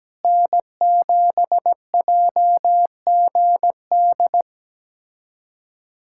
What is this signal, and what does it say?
Morse code, heard as a keyed tone, about 17 wpm, 700 Hz. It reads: N7JGD
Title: USA Callsign DE Callsign - 5 Characters - 17wpm